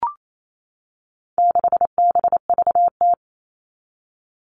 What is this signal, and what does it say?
Morse code, heard as a keyed tone, about 28 wpm, 700 Hz. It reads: 6BVT
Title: All Cut Numbers - 4-Digit 28wpm